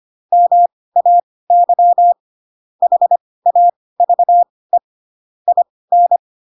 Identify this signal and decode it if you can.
Morse code, heard as a keyed tone, about 25 wpm, spaced as for 18 wpm, 700 Hz. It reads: MAY HAVE IN
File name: Pinocchio / Pinocchio ch7 12 25 ik1hge